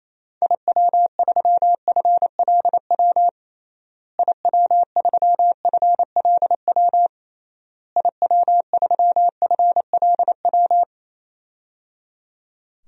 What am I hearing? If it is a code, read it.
Morse code, heard as a keyed tone, about 28 wpm, 700 Hz. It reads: IW3FLW IW3FLW IW3FLW
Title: DXCC-100 Call Signs - 6 Characters - Repeated 3x - 28wpm